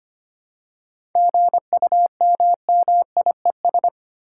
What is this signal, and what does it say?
Morse code, heard as a keyed tone, about 25 wpm, 700 Hz. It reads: GUMMIES
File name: practice / 25wpm / Easy Words with Hints - 25wpm